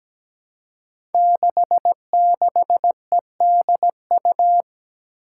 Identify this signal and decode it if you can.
Morse code, heard as a keyed tone, about 17 wpm, 700 Hz. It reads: 66EDU
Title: Five Challenging Letter-Number Combinations 17wpm